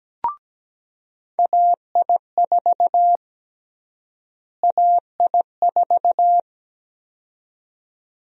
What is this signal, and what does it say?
Morse code, heard as a keyed tone, about 17 wpm, 700 Hz. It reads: AI4 AI4
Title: USA Call Sign Prefixes - Repeated 2x - 17wpm